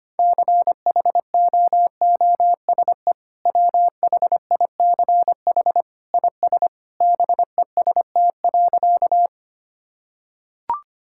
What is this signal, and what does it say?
Morse code, heard as a keyed tone, about 25 wpm, 700 Hz. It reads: CHOOSE WHICH IS BEST.
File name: practice / 25wpm / Sentences with 2-Word Phrases - 25wpm